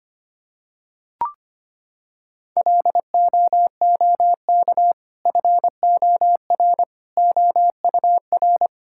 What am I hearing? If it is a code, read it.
Morse code, heard as a keyed tone, about 25 wpm, 700 Hz. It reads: LOOK FOR OUR
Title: Sentences from Top 1000 Words - Review Part 6 - 25wpm